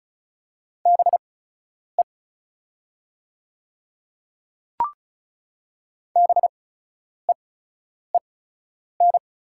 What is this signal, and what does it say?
Morse code, heard as a keyed tone, about 36 wpm, spaced as for 10 wpm, 700 Hz. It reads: BE BEEN
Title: 8x Character Spacing - Top 100 Words - 35wpm